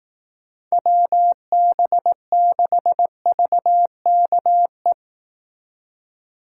Morse code, heard as a keyed tone, about 18 wpm, 700 Hz. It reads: WB6VKE
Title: Speed Racing - USA 2x3 Call Signs 15wpm